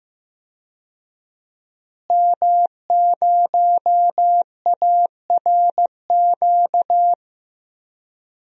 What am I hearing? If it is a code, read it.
Morse code, heard as a keyed tone, about 16 wpm, 700 Hz. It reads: M0ARQ
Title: DXCC-100 Call Signs - 5 Characters - Repeated 2x - 15wpm